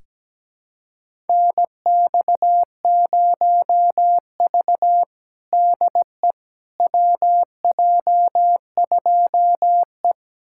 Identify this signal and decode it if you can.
Morse code, heard as a keyed tone, about 17 wpm, 700 Hz. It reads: NX0V DE WJ2E
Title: USA Callsign DE Callsign - 4 Characters - 17wpm